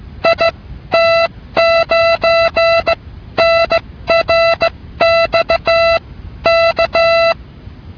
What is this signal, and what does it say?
Morse code, heard as a keyed tone, about 16 wpm, spaced as for 13 wpm, 670 Hz. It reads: IT9NRX K